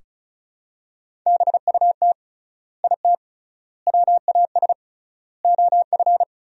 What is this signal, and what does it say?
Morse code, heard as a keyed tone, about 36 wpm, 700 Hz. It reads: BUT IT WAS OF
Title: 3x Word-Spacing - Sentences from Top 100 Words - 35wpm